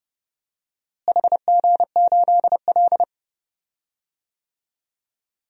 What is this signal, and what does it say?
Morse code, heard as a keyed tone, about 31 wpm, 700 Hz. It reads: HG8L
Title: DXCC-100 Call Signs - 4 Characters - Repeated 3x - 30wpm